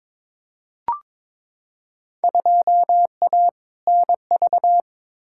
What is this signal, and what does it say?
Morse code, heard as a keyed tone, about 22 wpm, 700 Hz. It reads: 2A NV
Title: Field Day Exchanges - 22wpm